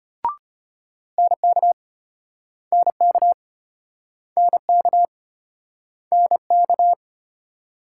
Morse code, heard as a keyed tone, about 33 wpm, 700 Hz. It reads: NK NK NK NK